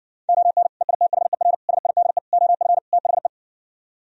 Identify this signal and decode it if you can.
Morse code, heard as a keyed tone, about 70 wpm, 700 Hz. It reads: ON EITHER SIDE OF THE